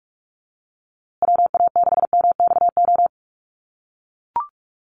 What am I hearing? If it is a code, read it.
Morse code, heard as a keyed tone, about 44 wpm, 700 Hz. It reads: WA6MXO
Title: USA Call Signs Repeated 2x 45wpm